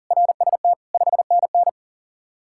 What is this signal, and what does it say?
Morse code, heard as a keyed tone, about 49 wpm, 700 Hz. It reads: RST 5NN
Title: Speed-Racing Common QSO Phrases - 40wpm